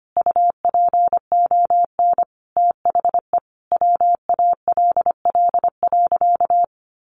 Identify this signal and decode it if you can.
Morse code, heard as a keyed tone, about 25 wpm, 700 Hz. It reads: UPON THE WALL.
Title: Sentences from Top 1100 Words 25wpm